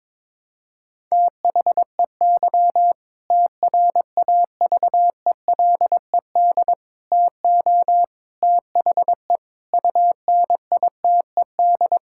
Morse code, heard as a keyed tone, about 22 wpm, 700 Hz. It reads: THEY TRAVELED TO THE UNITED
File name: practice / 22wpm / Sentences with 2-Word Phrases - 22wpm